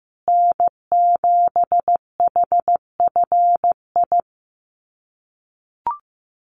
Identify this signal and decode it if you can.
Morse code, heard as a keyed tone, about 15 wpm, 700 Hz. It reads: N7HFI
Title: USA Call Signs Repeated 2x 15wpm